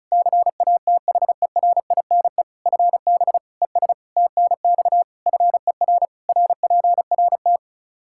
Morse code, heard as a keyed tone, about 34 wpm, 700 Hz. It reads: CATHERINE FB ES TNX FER RPRT
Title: Practice QSO - Exchange 2 - Info - 35wpm